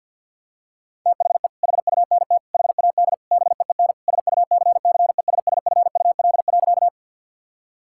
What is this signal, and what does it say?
Morse code, heard as a keyed tone, about 52 wpm, 700 Hz. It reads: THE HUNT HAD BEEN SUCCESSFUL.
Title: Sentences from Top 1600 Words 50wpm